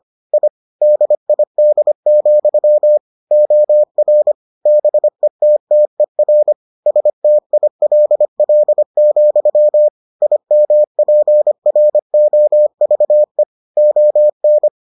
Morse code, heard as a keyed tone, about 25 wpm, 600 Hz. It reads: I DID, OR BETTER STILL, IMPROVE ON